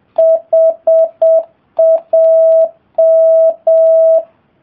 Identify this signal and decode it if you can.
Morse code, heard as a keyed tone, about 7 wpm, 660 Hz. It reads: HAM